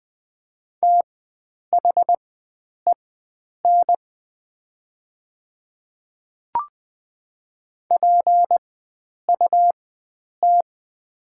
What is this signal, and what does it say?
Morse code, heard as a keyed tone, about 21 wpm, spaced as for 9 wpm, 700 Hz. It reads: THEN PUT
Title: 4x Character Spacing - Top 300 Words - 20wpm